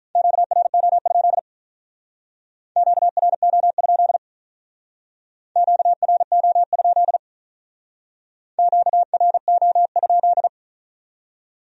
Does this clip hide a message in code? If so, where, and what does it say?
Yes: Morse code from 0.2 to 10.5 s, heard as a keyed tone, about 44 wpm, 700 Hz. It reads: QRO? QRO? QRO? QRO?